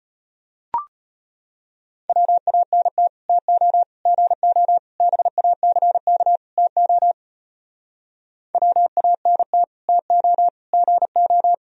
Morse code, heard as a keyed tone, about 33 wpm, 700 Hz. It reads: WANT TO GO BACK TO WANT TO GO